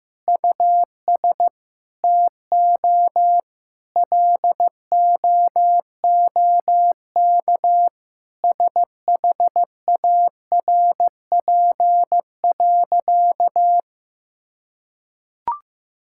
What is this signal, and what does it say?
Morse code, heard as a keyed tone, about 16 wpm, 700 Hz. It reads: US TO LOOK SHARP.
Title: Sentences from Top 1600 Words 15wpm